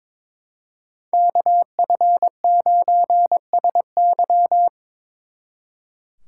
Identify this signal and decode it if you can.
Morse code, heard as a keyed tone, about 25 wpm, 700 Hz. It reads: KF9SY
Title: Speed Racing - USA 2x2 Call Signs 22wpm